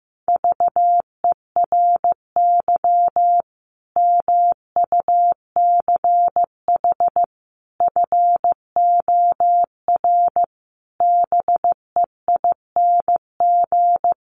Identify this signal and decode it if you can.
Morse code, heard as a keyed tone, about 15 wpm, 700 Hz. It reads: VERY MUCH FOR BEING
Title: Sets of 5 Words 15wpm